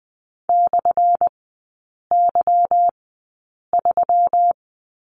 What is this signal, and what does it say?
Morse code, heard as a keyed tone, about 20 wpm, spaced as for 8 wpm, 700 Hz. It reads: /Y3
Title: 191 Lesson 31e - Letter v - Three Characters 20wpm